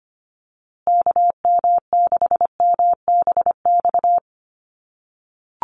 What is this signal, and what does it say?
Morse code, heard as a keyed tone, about 25 wpm, 700 Hz. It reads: KM6MBX
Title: USA Call Signs Repeated 3x 25wpm